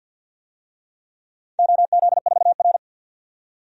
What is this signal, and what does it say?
Morse code, heard as a keyed tone, about 53 wpm, 700 Hz. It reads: OZ4R
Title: DXCC-100 Call Signs - 4 Characters - 50wpm